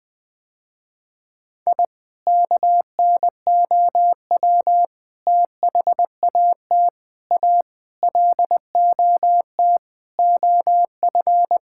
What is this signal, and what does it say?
Morse code, heard as a keyed tone, about 20 wpm, 690 Hz. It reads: I KNOW THAT A LOT OF
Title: Sentences with 4-Word Phrases - Spoken Beginning - 20wpm